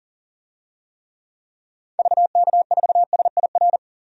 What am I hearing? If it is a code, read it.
Morse code, heard as a keyed tone, about 40 wpm, 700 Hz. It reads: VK4SIR
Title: DXCC-100 Call Signs - 6 Characters - 40wpm